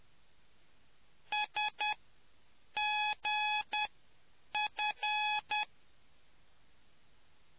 Morse code, heard as a keyed tone, about 10 wpm, spaced as for 7 wpm, 840 Hz. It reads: SGF